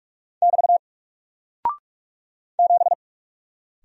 Morse code, heard as a keyed tone, about 44 wpm, 700 Hz. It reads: <BT> 7
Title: Single Character (Letter:Number:Punctuation:Prosign) 45wpm